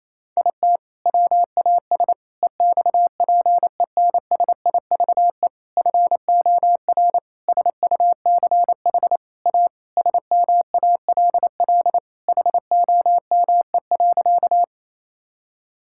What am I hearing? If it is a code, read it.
Morse code, heard as a keyed tone, about 28 wpm, 700 Hz. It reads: IT WAS EXPENSIVE FOR SUCH A SMALL HOME.
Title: Sentences from Top 1000 Words - Review Part 6 - 28wpm